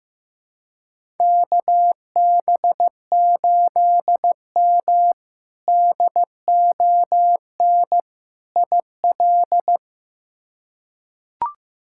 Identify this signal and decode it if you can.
Morse code, heard as a keyed tone, about 15 wpm, 700 Hz. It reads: KB8M DON IL
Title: K1USN SST QSO Practice - 15wpm